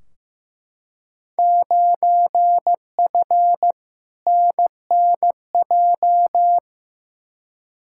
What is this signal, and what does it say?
Morse code, heard as a keyed tone, about 15 wpm, 700 Hz. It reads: 9F NNJ
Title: Field Day Exchanges Encore - 15wpm